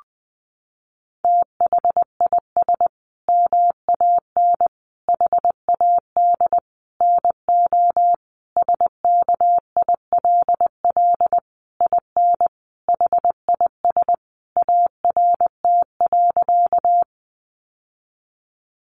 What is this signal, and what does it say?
Morse code, heard as a keyed tone, about 20 wpm, 700 Hz. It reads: THIS MAN HAD NO SKILL IN HIS ART.
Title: Sentences from Top 1200 Words 20wpm